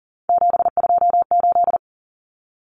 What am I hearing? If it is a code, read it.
Morse code, heard as a keyed tone, about 40 wpm, 700 Hz. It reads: 728
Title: Common Cut Numbers - 3-Digit 40wpm